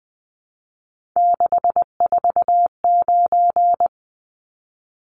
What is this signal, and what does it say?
Morse code, heard as a keyed tone, about 20 wpm, 700 Hz. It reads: 649
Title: Numbers - 3-Digit 20wpm